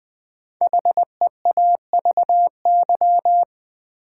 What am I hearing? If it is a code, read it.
Morse code, heard as a keyed tone, about 20 wpm, 700 Hz. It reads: HEAVY
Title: Sentences from Top 1100 Words - New Word Drill Alt - 20wpm